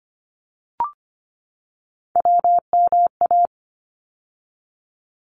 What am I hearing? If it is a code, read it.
Morse code, heard as a keyed tone, about 25 wpm, 700 Hz. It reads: WMA
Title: ARRL Sections 25wpm